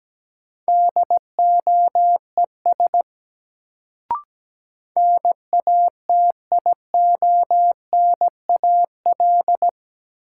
Morse code, heard as a keyed tone, about 17 wpm, 700 Hz. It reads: DOES NATIONAL
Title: Sets of 2 Words - New Word Drill - 17wpm